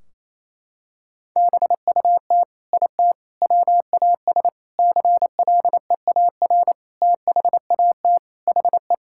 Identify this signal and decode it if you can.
Morse code, heard as a keyed tone, about 28 wpm, 700 Hz. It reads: BUT IT WAS CLEAR THAT HE
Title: Sentences from Top 1600 Words 28wpm